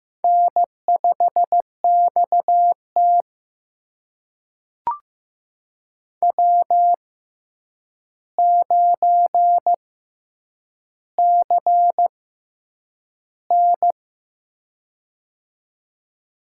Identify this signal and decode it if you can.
Morse code, heard as a keyed tone, about 15 wpm, spaced as for 8 wpm, 700 Hz. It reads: N5XT W9CN
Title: USA Call Signs - 4 Characters - 6x Character Spacing - 15wpm